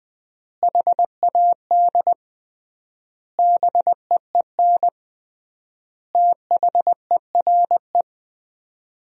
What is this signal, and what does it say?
Morse code, heard as a keyed tone, about 21 wpm, 700 Hz. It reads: HAD BEEN THERE
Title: 3x Word-Spacing - Sentences from Top 400 Words - 20wpm